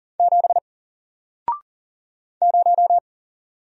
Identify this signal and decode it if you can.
Morse code, heard as a keyed tone, about 39 wpm, 700 Hz. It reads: 7 0
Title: Single Character (Letter:Number:Punctuation:Prosign) Alt 40wpm